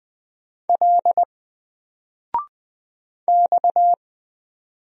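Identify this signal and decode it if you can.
Morse code, heard as a keyed tone, about 20 wpm, 700 Hz. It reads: L X
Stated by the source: Single letter and number Morse drill at 20 wpm, from N4LCD.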